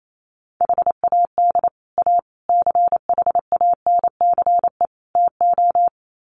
Morse code, heard as a keyed tone, about 28 wpm, 700 Hz. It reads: HAD A CHANCE TO
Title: Sets of 4 Words 28wpm